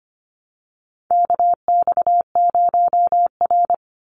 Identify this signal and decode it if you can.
Morse code, heard as a keyed tone, about 25 wpm, 700 Hz. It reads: KX0R
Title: NA SOTA Chaser Callsigns 25wpm